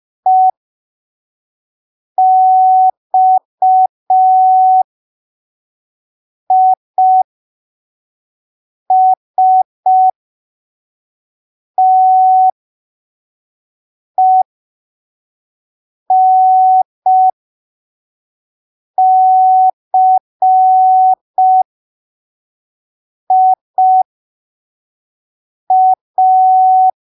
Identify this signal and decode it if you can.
Morse code, heard as a keyed tone, about 5 wpm, spaced as for 3 wpm, 750 Hz. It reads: EXISTENCIA
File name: arquivos / texto / texto 29 05ppm